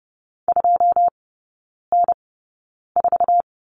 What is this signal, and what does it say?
Morse code, heard as a keyed tone, about 30 wpm, spaced as for 9 wpm, 700 Hz. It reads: 2N4
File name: 081 Lesson 14e - Number 2 - Three Characters 30wpm